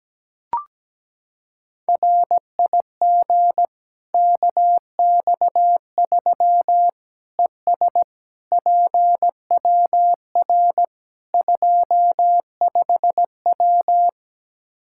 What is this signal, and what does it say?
Morse code, heard as a keyed tone, about 17 wpm, 700 Hz. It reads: RIG KX3 ES PWR 25W
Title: Practice QSO - Exchange 2 - Info - 17wpm